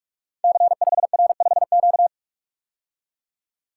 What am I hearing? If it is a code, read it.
Morse code, heard as a keyed tone, about 44 wpm, 700 Hz. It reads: C5R5Q
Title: Code Groups - 5 Letter:Numbers - 45wpm